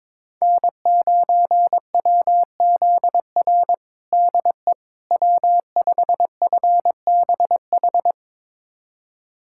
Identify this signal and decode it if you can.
Morse code, heard as a keyed tone, about 22 wpm, 700 Hz. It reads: N9WZR DE W5FBH
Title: USA Callsign DE Callsign - 5 Characters - 22wpm